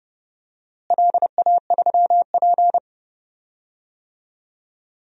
Morse code, heard as a keyed tone, about 30 wpm, 700 Hz. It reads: LA3P
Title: DXCC-100 Call Signs - 4 Characters - Repeated 3x - 30wpm